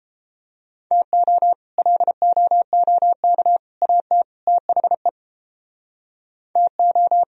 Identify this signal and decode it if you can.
Morse code, heard as a keyed tone, about 34 wpm, 700 Hz. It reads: TO LOOK AT THE TO